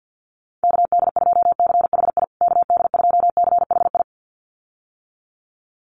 Morse code, heard as a keyed tone, about 50 wpm, 700 Hz. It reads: KD2CHI KD2CHI
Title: USA Call Signs Repeated 2x 50wpm